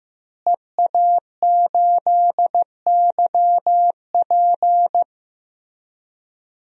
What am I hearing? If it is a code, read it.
Morse code, heard as a keyed tone, about 15 wpm, 700 Hz. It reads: EA8YP
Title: DXCC-100 Call Signs - 5 Characters - Repeated 3x - 15wpm